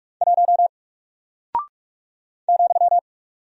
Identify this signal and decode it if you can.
Morse code, heard as a keyed tone, about 43 wpm, 700 Hz. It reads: 1 ,
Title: Single Character (Letter:Number:Punctuation:Prosign) Alt 45wpm